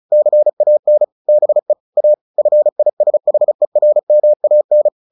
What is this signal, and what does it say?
Morse code, heard as a keyed tone, about 34 wpm, 600 Hz. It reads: CAN BE A FISHERMAN
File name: the old man and the sea / the old man and the sea 1 35wpm